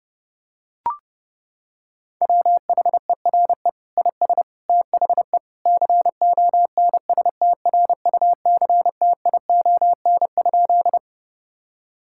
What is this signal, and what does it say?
Morse code, heard as a keyed tone, about 32 wpm, 700 Hz. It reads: WHERE IS THE CONSTRUCTION?
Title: Sentences from Top 1600 Words 30wpm